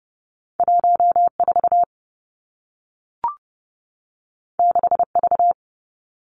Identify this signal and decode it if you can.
Morse code, heard as a keyed tone, about 30 wpm, 700 Hz. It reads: 1 4 6 V
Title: Two Challenging Letter-Number Combinations 30wpm